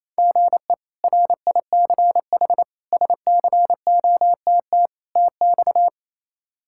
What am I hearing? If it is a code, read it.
Morse code, heard as a keyed tone, about 28 wpm, 700 Hz. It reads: GE RICH SCOTT TX
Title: K1USN SST QSO Practice - 28wpm